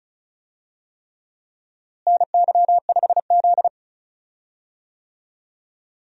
Morse code, heard as a keyed tone, about 36 wpm, 700 Hz. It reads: NY5Z
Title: USA Call Signs - 4 Characters - Repeated 3x - 35wpm